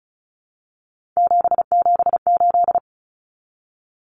Morse code, heard as a keyed tone, about 34 wpm, 700 Hz. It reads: 778
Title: Common Cut Numbers - 3-Digit 35wpm